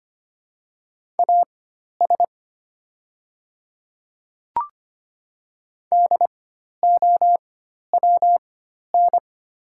Morse code, heard as a keyed tone, about 25 wpm, spaced as for 12 wpm, 700 Hz. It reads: AS DOWN
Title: 4x Character Spacing - Top 100 Words - 25wpm